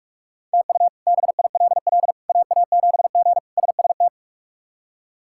Morse code, heard as a keyed tone, about 47 wpm, 700 Hz. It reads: TU BILL AA7G SST
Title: K1USN SST QSO Practice - 45wpm